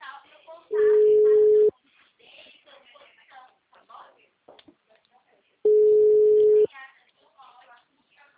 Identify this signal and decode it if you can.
Ringback tone (Europe, 425 Hz), with speech.